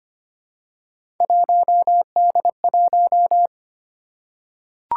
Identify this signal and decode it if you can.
Morse code, heard as a keyed tone, about 25 wpm, 700 Hz. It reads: 1D1
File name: Three Letter-Number Combinations - 25wpm